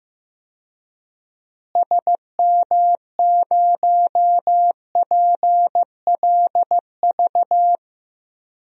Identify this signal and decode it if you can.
Morse code, heard as a keyed tone, about 16 wpm, 700 Hz. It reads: SM0PLV
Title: DXCC-100 Call Signs - 6 Characters - Repeated 2x - 15wpm